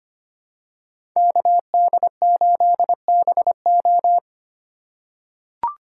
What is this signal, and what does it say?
Morse code, heard as a keyed tone, about 25 wpm, 700 Hz. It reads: KD8BO
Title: USA Call Signs - 5 Characters - 4x Character Spacing - 25wpm